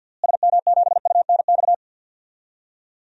Morse code, heard as a keyed tone, about 50 wpm, 700 Hz. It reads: SM7UNX
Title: DXCC-100 Call Signs - 6 Characters - Repeated 3x - 50wpm